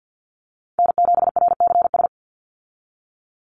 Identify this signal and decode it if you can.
Morse code, heard as a keyed tone, about 50 wpm, 700 Hz. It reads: N7RCS
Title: Call Signs Repeated 3x - 50wpm